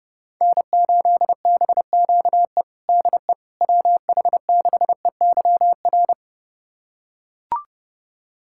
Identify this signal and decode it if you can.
Morse code, heard as a keyed tone, about 32 wpm, 700 Hz. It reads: N8BQE DE WH6EYR
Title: USA Callsign DE Callsign 30wpm